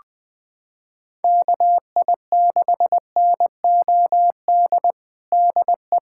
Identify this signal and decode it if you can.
Morse code, heard as a keyed tone, about 21 wpm, 700 Hz. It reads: KI6NOD DE
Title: USA Callsign DE Callsign - 6 Characters - 20wpm